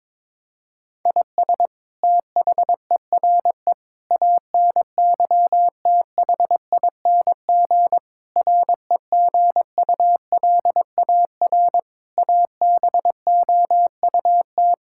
Morse code, heard as a keyed tone, about 22 wpm, 700 Hz. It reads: IS THERE ANYTHING REGULAR ABOUT
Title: Sentences from Top 1000 Words - Review Part 6 - 22wpm